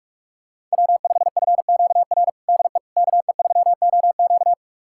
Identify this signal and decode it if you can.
Morse code, heard as a keyed tone, about 47 wpm, 700 Hz. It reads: W5FQR DE KE3OQ
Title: USA Callsign DE Callsign - 5 Characters - 45wpm